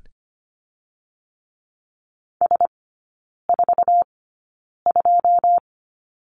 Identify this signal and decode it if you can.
Morse code, heard as a keyed tone, about 25 wpm, spaced as for 9 wpm, 690 Hz. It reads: S42
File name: Lesson 14e - Number 2 - Three Characters 25wpm